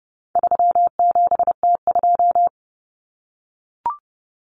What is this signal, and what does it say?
Morse code, heard as a keyed tone, about 29 wpm, 700 Hz. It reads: 37T2
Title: Common Cut Numbers - 4-Digit 30wpm